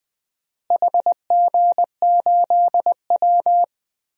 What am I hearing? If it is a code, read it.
Morse code, heard as a keyed tone, about 20 wpm, 700 Hz. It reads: HG8W
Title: DXCC-100 Call Signs - 4 Characters - Repeated 3x - 20wpm